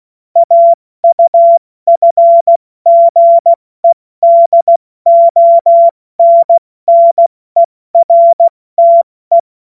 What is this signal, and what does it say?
Morse code, heard as a keyed tone, about 16 wpm, spaced as for 14 wpm, 680 Hz. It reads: AUFGEDONNERTE
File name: Morsekurs / Witze / Witz 31. August 2021, 70 F80